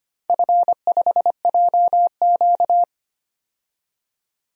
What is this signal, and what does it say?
Morse code, heard as a keyed tone, about 25 wpm, 690 Hz. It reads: F5JQ
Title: DXCC-100 Call Signs - 4 Characters - 25wpm